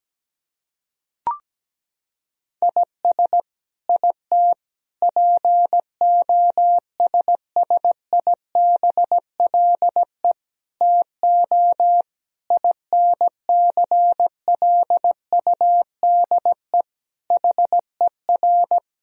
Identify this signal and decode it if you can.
Morse code, heard as a keyed tone, about 17 wpm, 700 Hz. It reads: IS IT POSSIBLE TO INCLUDE HER
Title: Sentences from Top 1000 Words - Review Part 3 - 17wpm